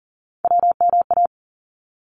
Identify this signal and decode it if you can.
Morse code, heard as a keyed tone, about 40 wpm, 700 Hz. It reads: WMA
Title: ARRL Sections 40wpm